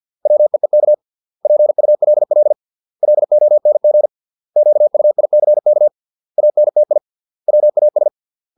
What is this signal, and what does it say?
Morse code, heard as a keyed tone, about 50 wpm, 600 Hz. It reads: WEEK PULL LONG QUICK ANTI WAS